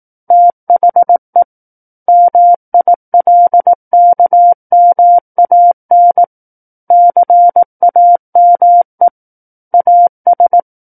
Morse code, heard as a keyed tone, about 18 wpm, 700 Hz. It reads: THE MILKMAN CAME AS
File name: 18 WPM